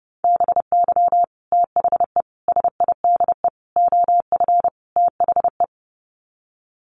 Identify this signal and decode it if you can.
Morse code, heard as a keyed tone, about 30 wpm, 700 Hz. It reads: BY THE SIDE OF THE
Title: Sets of 5 Words Encore2 30wpm